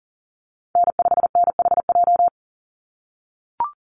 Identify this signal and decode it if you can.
Morse code, heard as a keyed tone, about 40 wpm, 700 Hz. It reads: N5NHJ
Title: Call Signs - 40wpm